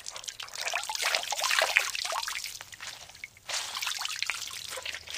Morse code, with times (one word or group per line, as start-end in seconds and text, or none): none